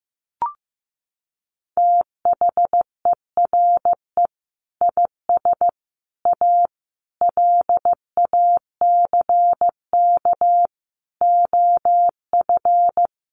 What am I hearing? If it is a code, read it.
Morse code, heard as a keyed tone, about 15 wpm, 700 Hz. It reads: THERE IS A LACK OF
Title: Sentences from Top 1100 Words 15wpm